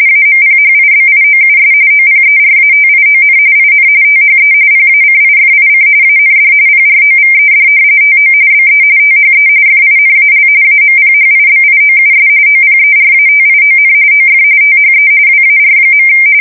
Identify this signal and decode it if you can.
RTTY, about 45.45 baud. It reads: AMPHIBIOUS OPERATION IN THE PACIFIC AND PROVED TO BE THE FINAL AMPHIBIOUS OPERATION OF THE WAR.
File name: fsk170 / battleshipnj